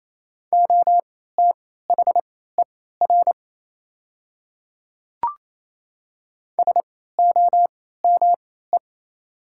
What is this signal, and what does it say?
Morse code, heard as a keyed tone, about 27 wpm, spaced as for 16 wpm, 690 Hz. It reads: OTHER SOME
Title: 3x Character Spacing - Top 100 Words - 28wpm